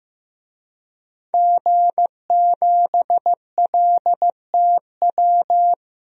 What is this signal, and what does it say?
Morse code, heard as a keyed tone, about 16 wpm, 700 Hz. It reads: G7LTW
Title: DXCC-100 Call Signs - 5 Characters - 15wpm